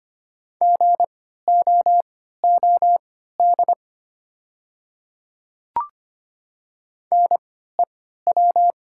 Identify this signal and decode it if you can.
Morse code, heard as a keyed tone, about 25 wpm, spaced as for 14 wpm, 700 Hz. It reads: GOOD NEW
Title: 3x Character Spacing - Top 200 Words - 25wpm